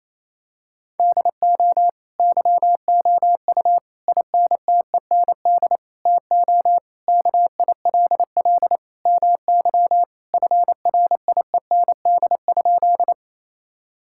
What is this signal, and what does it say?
Morse code, heard as a keyed tone, about 27 wpm, 700 Hz. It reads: DO YOU INTEND TO KILL MY FRIEND?
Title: Sentences from Top 1600 Words 28wpm